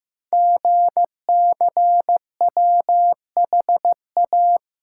Morse code, heard as a keyed tone, about 15 wpm, 700 Hz. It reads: GCWHA
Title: Code Groups - 5 Letter:Numbers - 15wpm